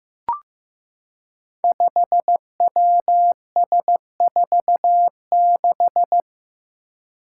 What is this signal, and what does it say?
Morse code, heard as a keyed tone, about 15 wpm, 700 Hz. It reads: 5WS46
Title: Five Challenging Letter-Number Combinations 15wpm